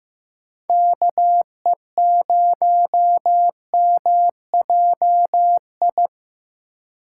Morse code, heard as a keyed tone, about 15 wpm, 700 Hz. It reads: KE0MJI
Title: USA Call Signs - 6 Characters - 3x Character Spacing - 15wpm